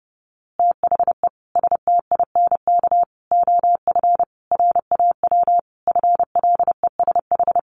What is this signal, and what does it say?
Morse code, heard as a keyed tone, about 28 wpm, 700 Hz. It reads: THE STINK OF RAW FLESH
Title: A Sound Of Thunder 30wpm